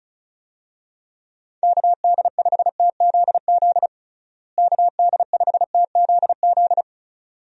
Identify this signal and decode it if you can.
Morse code, heard as a keyed tone, about 34 wpm, 700 Hz. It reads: KD5TZZ KD5TZZ
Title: USA Call Signs - 6 Characters - Repeated 2x - 35wpm